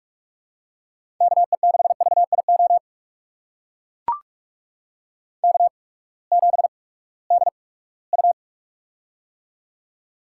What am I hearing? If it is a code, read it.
Morse code, heard as a keyed tone, about 41 wpm, spaced as for 22 wpm, 700 Hz. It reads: KE6VIO K7DU
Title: USA Call Signs - 8x Character Spacing - 45wpm